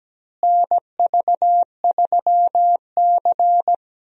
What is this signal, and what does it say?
Morse code, heard as a keyed tone, about 17 wpm, 700 Hz. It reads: NV3C
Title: USA Call Signs - 4 Characters - Repeated 2x - 17wpm